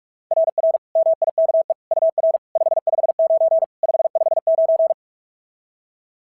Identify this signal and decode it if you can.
Morse code, heard as a keyed tone, about 44 wpm, 690 Hz. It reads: RR MIKE UR 559 559
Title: SOTA QSO Practice 45wpm